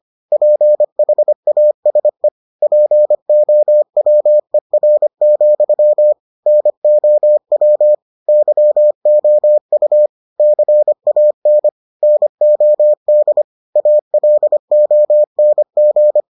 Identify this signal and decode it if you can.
Morse code, heard as a keyed tone, about 25 wpm, 600 Hz. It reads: PHASE POWER, NOW YOU CAN NOD ALONG